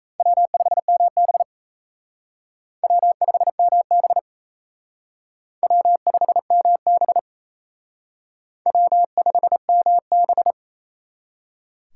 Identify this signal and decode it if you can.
Morse code, heard as a keyed tone, about 33 wpm, 700 Hz. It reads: W5MB W5MB W5MB W5MB
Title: Speed Racing - USA Call Signs 28wpm